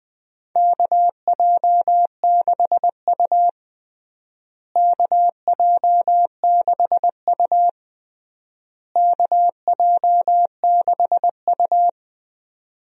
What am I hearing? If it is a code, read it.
Morse code, heard as a keyed tone, about 20 wpm, 700 Hz. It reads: KJ6U KJ6U KJ6U
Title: USA Call Signs - 4 Characters - Repeated 3x - 20wpm